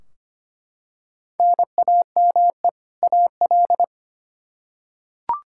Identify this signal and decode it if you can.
Morse code, heard as a keyed tone, about 25 wpm, 700 Hz. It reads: NAME AL
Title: Practice QSO - Exchange 1 - Beginning - 25wpm